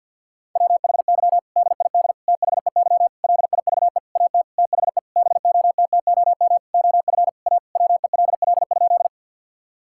Morse code, heard as a keyed tone, about 50 wpm, 700 Hz. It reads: WHY DID THEY LIVE AT THE BOTTOM OF A WELL?
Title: Sentences from Top 1000 Words - Review Part 6 - 50wpm